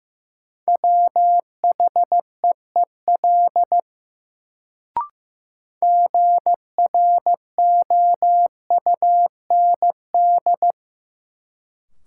Morse code, heard as a keyed tone, about 16 wpm, 700 Hz. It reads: WHEEL GROUND
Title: Sentences from Top 500 Words - New Word Drill - 15wpm